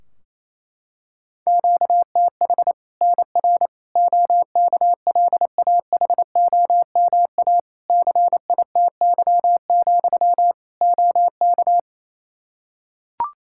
Morse code, heard as a keyed tone, about 28 wpm, 700 Hz. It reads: QTH NR OKLAHOMA CITY, OK
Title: Practice QSO - Exchange 1 - Beginning - 28wpm